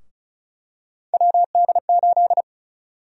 Morse code, heard as a keyed tone, about 36 wpm, 700 Hz. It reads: WD8
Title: USA Call Sign Prefixes - 35wpm